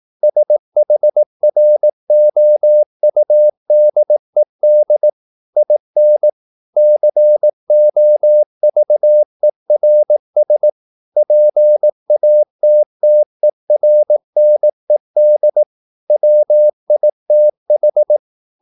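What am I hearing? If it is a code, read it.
Morse code, heard as a keyed tone, about 18 wpm, 600 Hz. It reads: SHROUDED IN COVERS PATTERNED WITH
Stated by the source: Morse practice at 18 wpm, English text, from CW med Gnister.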